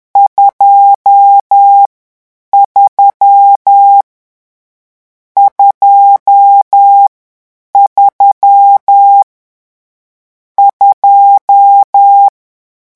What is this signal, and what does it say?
Morse code, heard as a keyed tone, about 10 wpm, 800 Hz. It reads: 2 3 2 3 2
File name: morse mp3 / lecon10-vitesse 10